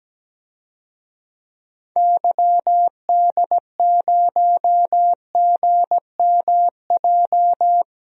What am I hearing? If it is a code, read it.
Morse code, heard as a keyed tone, about 17 wpm, 700 Hz. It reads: YD0GMJ
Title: DXCC-100 Call Signs - 6 Characters - Repeated 2x - 17wpm